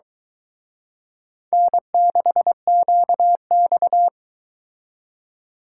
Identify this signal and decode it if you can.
Morse code, heard as a keyed tone, about 23 wpm, 700 Hz. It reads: N6QX